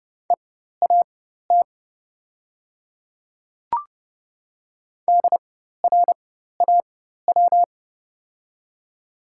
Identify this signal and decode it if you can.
Morse code, heard as a keyed tone, about 30 wpm, spaced as for 14 wpm, 700 Hz. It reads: EAT DRAW
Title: 4x Character Spacing - Top 300 Words - 30wpm